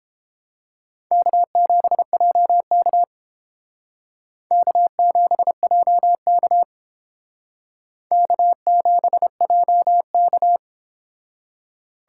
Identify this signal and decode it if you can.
Morse code, heard as a keyed tone, about 29 wpm, 700 Hz. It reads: K7JK K7JK K7JK